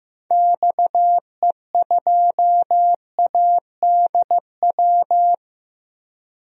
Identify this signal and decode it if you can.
Morse code, heard as a keyed tone, about 16 wpm, 700 Hz. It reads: XE2ADW
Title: DXCC-100 Call Signs - 6 Characters - Repeated 3x - 15wpm